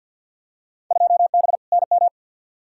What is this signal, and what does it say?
Morse code, heard as a keyed tone, about 50 wpm, 700 Hz. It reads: 2B NM
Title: Field Day Exchanges Encore - 50wpm